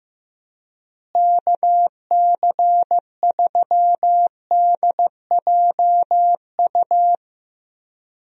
Morse code, heard as a keyed tone, about 15 wpm, 700 Hz. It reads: KC3DJU
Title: USA Call Signs - 6 Characters - Repeated 3x - 15wpm